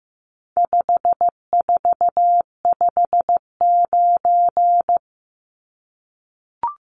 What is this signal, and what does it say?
Morse code, heard as a keyed tone, about 15 wpm, 700 Hz. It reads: 5459
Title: Numbers - 4-Digit 15wpm